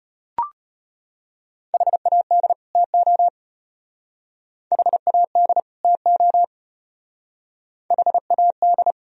Morse code, heard as a keyed tone, about 31 wpm, 700 Hz. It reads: HAD TO HAD TO HAD
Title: Speed-Racing - Sets of 2 Words 25wpm